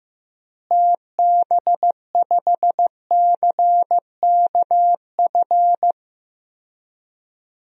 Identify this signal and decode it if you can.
Morse code, heard as a keyed tone, about 15 wpm, 700 Hz. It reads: TB5CKF
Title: DXCC-100 Call Signs - 6 Characters - Repeated 2x - 15wpm